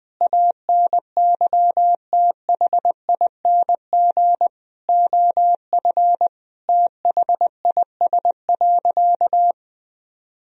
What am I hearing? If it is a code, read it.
Morse code, heard as a keyed tone, about 20 wpm, 700 Hz. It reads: ANYTHING OF THIS.
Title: Sentences from Top 1000 Words - Review Part 3 - 20wpm